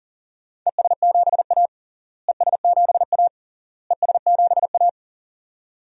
Morse code, heard as a keyed tone, about 40 wpm, 700 Hz. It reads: ES7A ES7A ES7A
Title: DXCC-100 Call Signs - 4 Characters - Repeated 3x - 40wpm